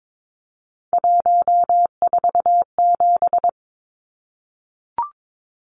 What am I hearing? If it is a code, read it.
Morse code, heard as a keyed tone, about 22 wpm, 700 Hz. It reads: 147
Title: Numbers - 3-Digit 22wpm